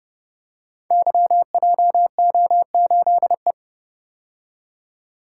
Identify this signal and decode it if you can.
Morse code, heard as a keyed tone, about 30 wpm, 700 Hz. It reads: YJO8E
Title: Code Groups - 5 Letter:Numbers - 30wpm